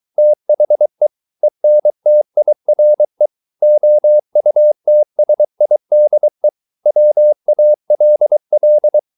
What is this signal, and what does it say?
Morse code, heard as a keyed tone, about 23 wpm, 600 Hz. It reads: THE ENTIRE OUTSIDE WALL